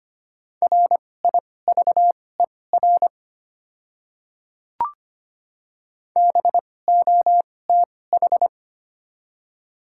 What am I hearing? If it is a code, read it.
Morse code, heard as a keyed tone, about 25 wpm, spaced as for 18 wpm, 700 Hz. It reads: RIVER BOTH
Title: 2x Character Spacing - Top 300 Words - 25wpm